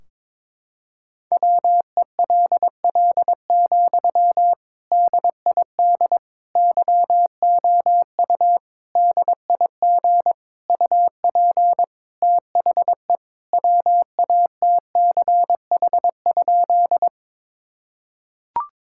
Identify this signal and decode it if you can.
Morse code, heard as a keyed tone, about 22 wpm, 700 Hz. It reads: WELL, DID YOU DIG UP THE WATCH?
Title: Sentences from Top 1600 Words 22wpm